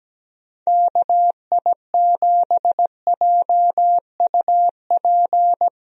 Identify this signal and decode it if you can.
Morse code, heard as a keyed tone, about 17 wpm, 700 Hz. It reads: KI7JUP
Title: USA Call Signs - 6 Characters - Repeated 2x - 17wpm